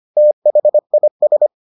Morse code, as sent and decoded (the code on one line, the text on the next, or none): - .... .. ...
THIS